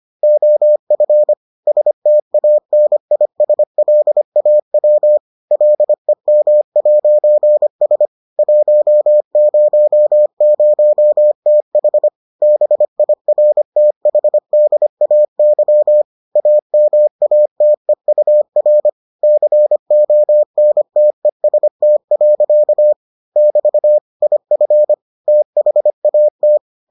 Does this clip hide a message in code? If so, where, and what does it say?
Yes: Morse code from 0.2 to 26.6 s, heard as a keyed tone, about 25 wpm, 600 Hz. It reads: OF STANISLAW LEM'S 100TH BIRTHDAY AMATEUR CONTEST. <BT> IF THAT